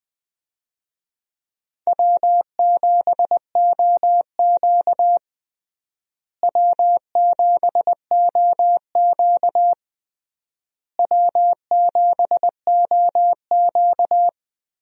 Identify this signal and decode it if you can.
Morse code, heard as a keyed tone, about 20 wpm, 700 Hz. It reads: W7OQ W7OQ W7OQ
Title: USA Call Signs - 4 Characters - Repeated 3x - 20wpm